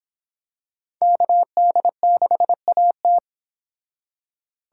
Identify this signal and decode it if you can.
Morse code, heard as a keyed tone, about 25 wpm, 700 Hz. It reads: KD6AT